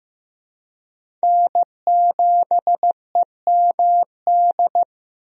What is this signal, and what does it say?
Morse code, heard as a keyed tone, about 15 wpm, 700 Hz. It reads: N7EMD
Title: USA Call Signs - 5 Characters - Repeated 3x - 15wpm